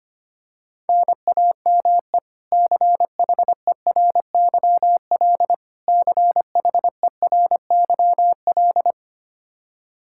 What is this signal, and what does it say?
Morse code, heard as a keyed tone, about 25 wpm, 700 Hz. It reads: NAME CHERYL CHERYL
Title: K1USN SST QSO Practice 25wpm